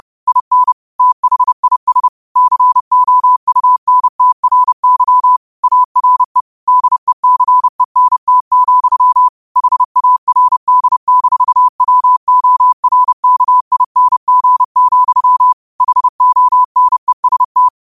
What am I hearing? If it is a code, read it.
Morse code, heard as a keyed tone, about 30 wpm, 1000 Hz. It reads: IN THIS COUNTRY ARE DECENT, HARD-WORKING, HONEST